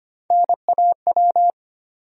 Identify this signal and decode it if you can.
Morse code, heard as a keyed tone, about 25 wpm, 700 Hz. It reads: NAW
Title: USA Call Sign Postfixes - 3 Letters 25wpm